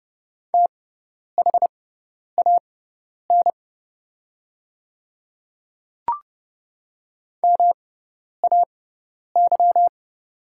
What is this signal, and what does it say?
Morse code, heard as a keyed tone, about 30 wpm, spaced as for 10 wpm, 700 Hz. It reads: THAN MAY
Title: 6x Character Spacing - Top 200 Words - 30wpm